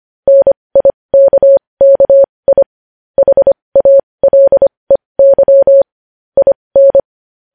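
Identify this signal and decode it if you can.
Morse code, heard as a keyed tone, about 25 wpm, spaced as for 20 wpm, 550 Hz. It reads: NIKKI HALEY IN
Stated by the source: Morse code news headlines practice at 20 wpm.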